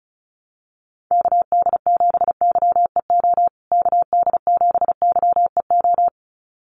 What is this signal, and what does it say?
Morse code, heard as a keyed tone, about 34 wpm, 700 Hz. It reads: KD7YEO KD7YEO
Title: USA Call Signs Repeated 2x 35wpm